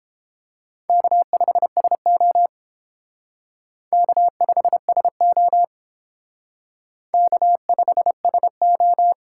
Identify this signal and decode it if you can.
Morse code, heard as a keyed tone, about 29 wpm, 700 Hz. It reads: K5SO K5SO K5SO